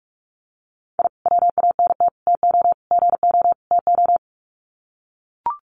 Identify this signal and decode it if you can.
Morse code, heard as a keyed tone, about 44 wpm, 700 Hz. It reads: I WANT TO GO TO
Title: Sets of 5 Words Encore 45wpm